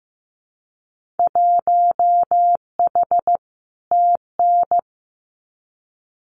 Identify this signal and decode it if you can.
Morse code, heard as a keyed tone, about 15 wpm, 700 Hz. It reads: 1H TN
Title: Winter Field Day Exchanges 15wpm